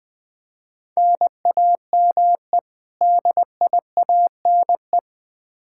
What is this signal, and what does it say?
Morse code, heard as a keyed tone, about 20 wpm, 700 Hz. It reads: NAME DIANE
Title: K1USN SST QSO Practice 20wpm